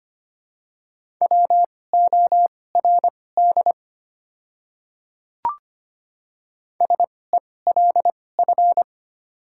Morse code, heard as a keyed tone, about 25 wpm, spaced as for 18 wpm, 700 Hz. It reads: WORD SELF
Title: 2x Character Spacing - Top 200 Words - 25wpm